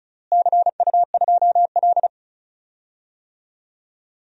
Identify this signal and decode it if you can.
Morse code, heard as a keyed tone, about 36 wpm, 690 Hz. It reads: CU2L
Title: DXCC-100 Call Signs - 4 Characters - Repeated 3x - 35wpm